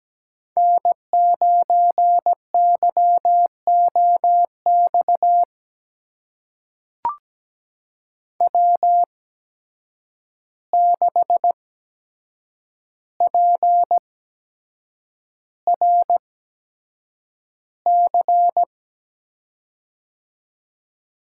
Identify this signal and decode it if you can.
Morse code, heard as a keyed tone, about 17 wpm, spaced as for 7 wpm, 700 Hz. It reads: N9YOX W6PRC
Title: USA Call Signs - 5 Characters - 8x Character Spacing - 17wpm